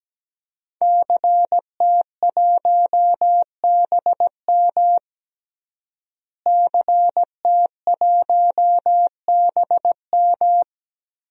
Morse code, heard as a keyed tone, about 17 wpm, 700 Hz. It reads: CT1BM CT1BM
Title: DXCC-100 Call Signs - 5 Characters - Repeated 2x - 17wpm